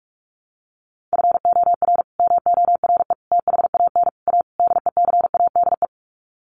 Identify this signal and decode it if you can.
Morse code, heard as a keyed tone, about 45 wpm, 700 Hz. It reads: FOR MORE THAN A DECADE
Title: Sets of 5 Words 45wpm